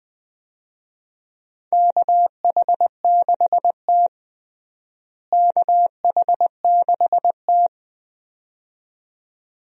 Morse code, heard as a keyed tone, about 21 wpm, 700 Hz. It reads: KH6T KH6T
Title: DXCC-100 Call Signs - 4 Characters - Repeated 2x - 20wpm